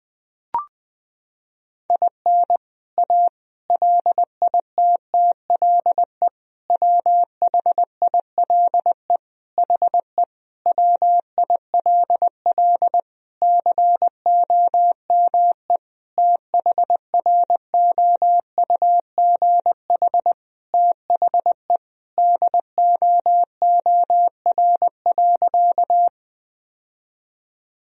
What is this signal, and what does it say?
Morse code, heard as a keyed tone, about 21 wpm, 700 Hz. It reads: IN A LITTLE WHILE HE WILL COME THROUGH THE DOOR.
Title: Sentences from Top 1000 Words - Review Part 1 - 20wpm